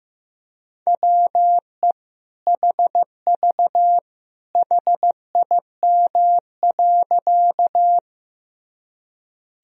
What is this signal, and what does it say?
Morse code, heard as a keyed tone, about 15 wpm, 700 Hz. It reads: WE HV HIM.
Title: Sentences from Top 100 Words and Abbreviations 15wpm